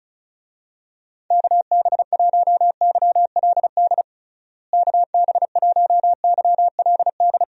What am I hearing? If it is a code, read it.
Morse code, heard as a keyed tone, about 34 wpm, 700 Hz. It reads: KB1YLD KB1YLD
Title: USA Call Signs - 6 Characters - Repeated 2x - 35wpm